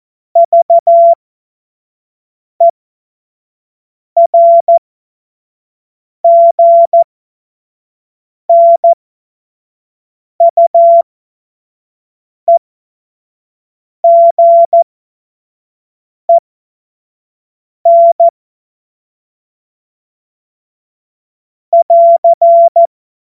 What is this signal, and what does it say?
Morse code, heard as a keyed tone, about 14 wpm, spaced as for 5 wpm, 680 Hz. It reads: VERGNUEGEN <AR>
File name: Morsekurs / Witze / Witz 09 April 2024 25 F70